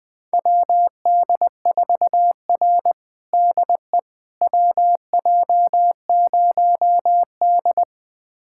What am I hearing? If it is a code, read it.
Morse code, heard as a keyed tone, about 21 wpm, 700 Hz. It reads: WD4R DE WJ0D
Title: USA Callsign DE Callsign - 4 Characters - 20wpm